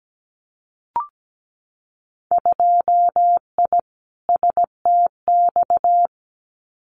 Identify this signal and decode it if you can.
Morse code, heard as a keyed tone, about 17 wpm, 690 Hz. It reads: 2I STX
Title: Winter Field Day Exchanges 17wpm